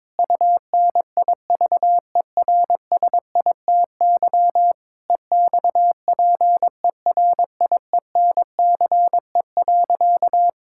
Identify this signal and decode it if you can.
Morse code, heard as a keyed tone, about 21 wpm, 700 Hz. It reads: UNIVERSITY EXPERIENCE.
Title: Sentences from Top 1000 Words - Review Part 5 - 22wpm